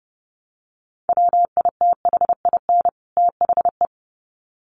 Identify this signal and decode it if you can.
Morse code, heard as a keyed tone, about 30 wpm, 700 Hz. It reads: WITHIN THE
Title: Sets of 2 Words 30wpm